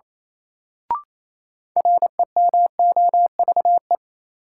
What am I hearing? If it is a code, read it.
Morse code, heard as a keyed tone, about 28 wpm, 700 Hz. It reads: REMOVE
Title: Sentences from Top 800 Words - New Word Drill - 28wpm